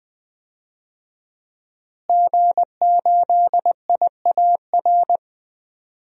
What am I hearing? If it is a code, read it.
Morse code, heard as a keyed tone, about 20 wpm, 700 Hz. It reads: G8IAR
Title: DXCC-100 Call Signs - 5 Characters - Repeated 2x - 20wpm